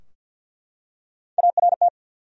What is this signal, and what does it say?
Morse code, heard as a keyed tone, about 48 wpm, 700 Hz. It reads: ART
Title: USA Common Names - Max 3 Letters - 50wpm